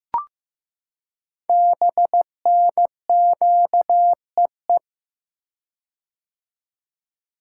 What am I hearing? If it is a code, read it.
Morse code, heard as a keyed tone, about 15 wpm, 700 Hz. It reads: BNQEE
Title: Code Groups - 5 Letters - 15wpm